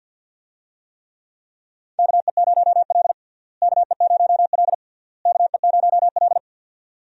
Morse code, heard as a keyed tone, about 48 wpm, 700 Hz. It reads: KE0L KE0L KE0L
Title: USA Call Signs - 4 Characters - Repeated 3x - 50wpm